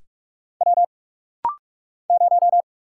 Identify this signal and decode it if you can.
Morse code, heard as a keyed tone, about 44 wpm, 700 Hz. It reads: W 0
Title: Single Letter-Number - Rapid-Fire 45wpm